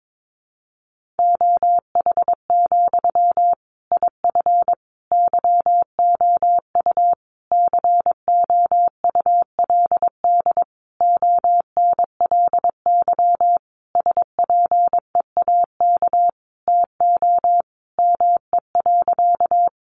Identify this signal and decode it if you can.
Morse code, heard as a keyed tone, about 22 wpm, 700 Hz. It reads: OH, IF YOU COULD ONLY SPEAK TO ME.
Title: Sentences from Top 1100 Words 22wpm